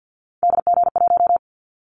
Morse code, heard as a keyed tone, about 50 wpm, 700 Hz. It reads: DG1
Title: Callsign Prefixes for Top 50 DXCC - 50wpm